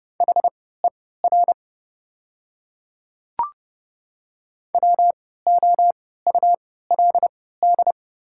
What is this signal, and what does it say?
Morse code, heard as a keyed tone, about 32 wpm, spaced as for 17 wpm, 700 Hz. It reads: HER WOULD
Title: 3x Character Spacing - Top 300 Words - 30wpm